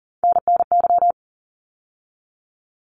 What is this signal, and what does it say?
Morse code, heard as a keyed tone, about 40 wpm, 700 Hz. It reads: NNY
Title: ARRL Sections 40wpm